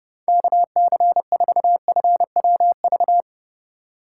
Morse code, heard as a keyed tone, about 31 wpm, 700 Hz. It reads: KC4FWV